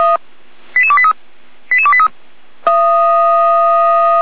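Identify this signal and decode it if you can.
Five-tone selective call (ZVEI), heard as quick runs of five tones, tones of about 70 ms. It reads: 88283 88283
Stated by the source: ZVEI five-tone selective call received on an SDR.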